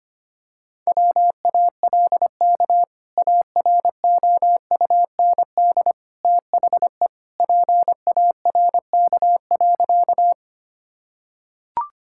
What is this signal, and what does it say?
Morse code, heard as a keyed tone, about 25 wpm, 700 Hz. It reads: WALK AROUND THE PARK.
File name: practice / 25wpm / Sentences with 2-Word Phrases - 25wpm